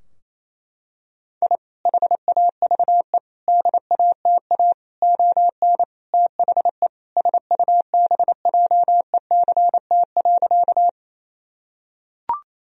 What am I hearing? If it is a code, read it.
Morse code, heard as a keyed tone, about 27 wpm, 700 Hz. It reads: I HAVE DATA ON THE SUBJECT.
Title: Sentences from Top 1000 Words - Review Part 3 - 28wpm